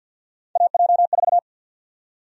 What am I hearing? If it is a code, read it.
Morse code, heard as a keyed tone, about 49 wpm, 700 Hz. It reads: AJ4
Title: USA Call Sign Prefixes - 50wpm